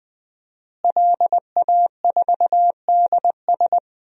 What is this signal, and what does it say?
Morse code, heard as a keyed tone, about 20 wpm, 700 Hz. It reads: LA4DS
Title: DXCC-100 Call Signs - 5 Characters - Repeated 3x - 20wpm